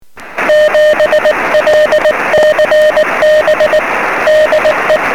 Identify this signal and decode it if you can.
Morse code, heard as a keyed tone, about 19 wpm, 640 Hz, with noise. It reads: 7LCB DE
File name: sm7lcb / 09-dl0vv 23cm